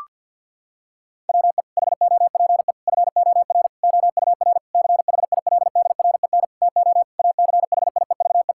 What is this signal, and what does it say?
Morse code, heard as a keyed tone, about 47 wpm, 700 Hz. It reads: WE HOPE FOR OUR CHILDREN TO ACHIEVE
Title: Sentences from Top 1000 Words - Review Part 6 - 50wpm